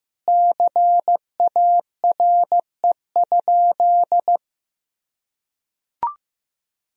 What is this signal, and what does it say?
Morse code, heard as a keyed tone, about 15 wpm, 700 Hz. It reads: CARE?
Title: Sentences from Top 1000 Words - Review Part 1 - 15wpm